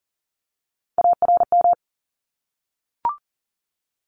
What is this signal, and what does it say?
Morse code, heard as a keyed tone, about 40 wpm, 700 Hz. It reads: ARM
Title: ES:S Suffix - Top 100 Words with Base Words 40wpm